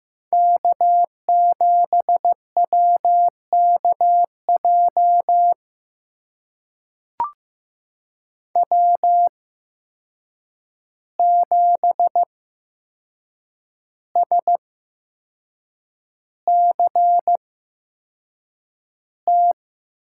Morse code, heard as a keyed tone, about 15 wpm, spaced as for 6 wpm, 700 Hz. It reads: K7WKJ W7SCT
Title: USA Call Signs - 5 Characters - 8x Character Spacing - 15wpm